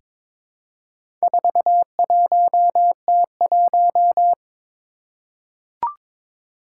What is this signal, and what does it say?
Morse code, heard as a keyed tone, about 22 wpm, 700 Hz. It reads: 41T1
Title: Four Challenging Letter-Number Combinations 22wpm